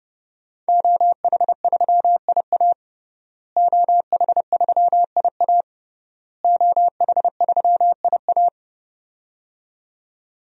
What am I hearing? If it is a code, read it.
Morse code, heard as a keyed tone, about 30 wpm, 700 Hz. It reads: OH3IA OH3IA OH3IA
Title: DXCC-100 Call Signs - 5 Characters - Repeated 3x - 30wpm